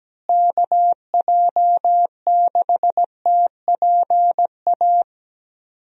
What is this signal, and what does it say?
Morse code, heard as a keyed tone, about 17 wpm, 700 Hz. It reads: KJ6TPA
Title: USA Call Signs - 6 Characters - Repeated 2x - 17wpm